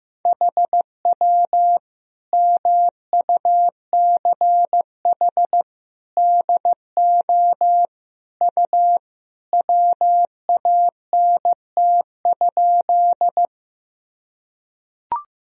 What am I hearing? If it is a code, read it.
Morse code, heard as a keyed tone, about 15 wpm, 700 Hz. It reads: HW MUCH DO U WANT?
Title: Sentences from Top 200 Words and Abbreviations 15wpm